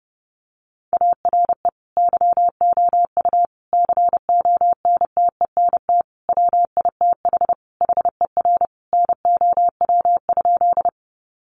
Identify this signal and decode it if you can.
Morse code, heard as a keyed tone, about 30 wpm, 700 Hz. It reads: ARE YOU CONTENT WITH HER NOW?
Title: Sentences from Top 900 Words 30wpm